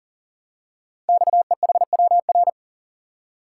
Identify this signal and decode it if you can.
Morse code, heard as a keyed tone, about 39 wpm, 700 Hz. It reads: XEHWR
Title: Code Groups - 5 Letters - 40wpm